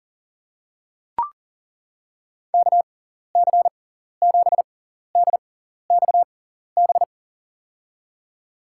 Morse code, heard as a keyed tone, about 40 wpm, spaced as for 14 wpm, 700 Hz. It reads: KC7DXB
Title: USA Call Signs - 6 Characters - 6x Character Spacing - 40wpm